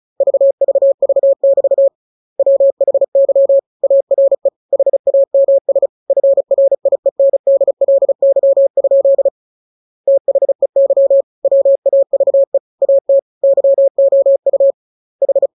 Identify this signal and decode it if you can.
Morse code, heard as a keyed tone, about 34 wpm, 550 Hz. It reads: VVV<BT> WHY ARE HAMS FRIENDLY? THEY WAVE AT YOU H
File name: jokes / joke 23 35 0000